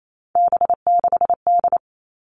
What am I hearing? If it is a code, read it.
Morse code, heard as a keyed tone, about 28 wpm, 700 Hz. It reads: B6D
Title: All Cut Numbers - 3-Digit 28wpm